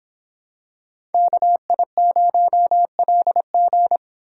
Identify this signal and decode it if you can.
Morse code, heard as a keyed tone, about 26 wpm, 700 Hz. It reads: KI0LG